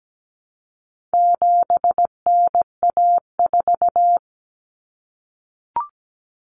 Morse code, heard as a keyed tone, about 17 wpm, 700 Hz. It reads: 7NA4
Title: Common Cut Numbers - 4-Digit 17wpm